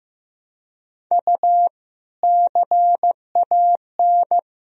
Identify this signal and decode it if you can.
Morse code, heard as a keyed tone, about 15 wpm, 700 Hz. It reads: U CAN
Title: Sentences from Top 100 Words and Abbreviations 15wpm